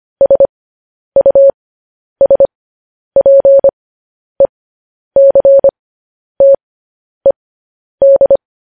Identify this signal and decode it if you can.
Morse code, heard as a keyed tone, about 25 wpm, spaced as for 10 wpm, 550 Hz. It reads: SUSPECTED